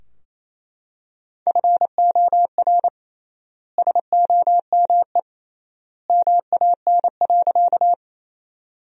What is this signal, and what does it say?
Morse code, heard as a keyed tone, about 27 wpm, 700 Hz. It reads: FOR SOME MAN.
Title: 3x Word-Spacing - Sentences from Top 200 Words - 28wpm